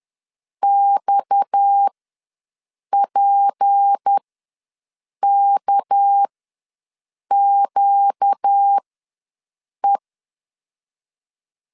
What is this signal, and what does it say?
Morse code, heard as a keyed tone, about 10 wpm, spaced as for 6 wpm, 800 Hz. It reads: XPKQE
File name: LEZIONI / 6 PKX0 / 6 PKX ws 10wpm 3706